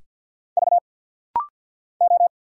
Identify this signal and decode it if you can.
Morse code, heard as a keyed tone, about 50 wpm, 700 Hz. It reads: V O
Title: Single Letters - Rapid-Fire 50wpm